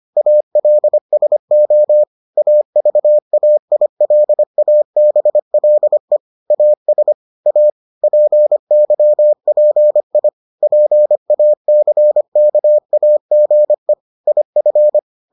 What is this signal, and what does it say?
Morse code, heard as a keyed tone, about 25 wpm, 600 Hz. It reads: ALSO AVAILABLE AS A PYPI PACKAGE IF